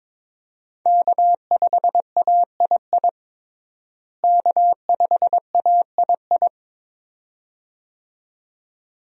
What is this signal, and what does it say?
Morse code, heard as a keyed tone, about 22 wpm, 700 Hz. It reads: K5AII K5AII
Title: USA Call Signs - 5 Characters - Repeated 2x - 22wpm